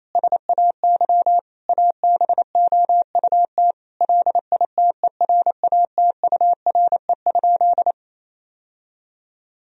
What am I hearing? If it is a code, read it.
Morse code, heard as a keyed tone, about 28 wpm, 700 Hz. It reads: SAY ABOUT LITERATURE?
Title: Sentences from Top 1000 Words - Review Part 3 - 28wpm